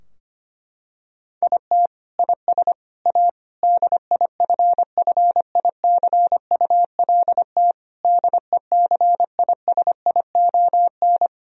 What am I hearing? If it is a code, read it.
Morse code, heard as a keyed tone, about 25 wpm, 700 Hz. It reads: IT IS A DIFFICULT DECISION
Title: Sentences from Top 1000 Words - Review Part 4 - 25wpm